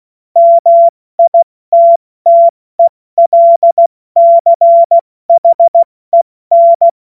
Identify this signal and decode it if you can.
Morse code, heard as a keyed tone, about 16 wpm, 680 Hz. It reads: MITTELCHEN